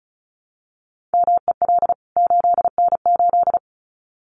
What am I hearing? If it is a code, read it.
Morse code, heard as a keyed tone, about 34 wpm, 700 Hz. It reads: MEL 8N8
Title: CWT Exchange 35wpm